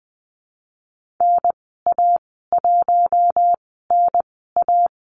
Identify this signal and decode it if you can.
Morse code, heard as a keyed tone, about 20 wpm, spaced as for 14 wpm, 700 Hz. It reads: NA1NA
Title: 048 Lesson 9g - Number 4 - Call Signs 20wpm